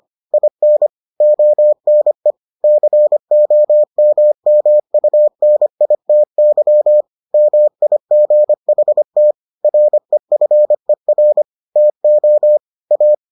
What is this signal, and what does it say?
Morse code, heard as a keyed tone, about 25 wpm, 600 Hz. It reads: IN ONE COMMUNITY MIGHT REFER TO A